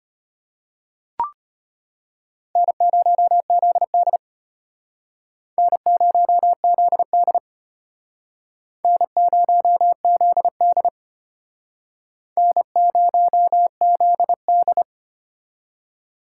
Morse code, heard as a keyed tone, about 33 wpm, 700 Hz. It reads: N0ZD N0ZD N0ZD N0ZD